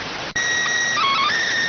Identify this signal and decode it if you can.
SSTV (slow-scan television), PD 290.